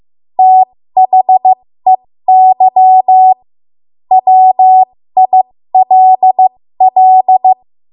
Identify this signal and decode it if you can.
Morse code, heard as a keyed tone, about 15 wpm, spaced as for 13 wpm, 750 Hz. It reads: THEY WILL
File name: Morse / 13WPM